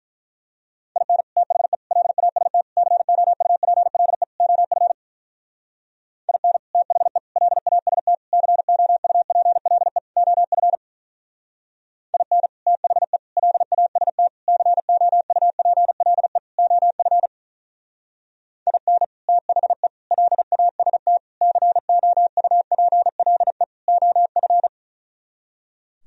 Morse code, heard as a keyed tone, about 42 wpm, 700 Hz. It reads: IN THE LAST COUPLE OF IN THE LAST COUPLE OF IN THE LAST COUPLE OF IN THE LAST COUPLE OF
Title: Speed-Racing - Sets of 5 Words 35wpm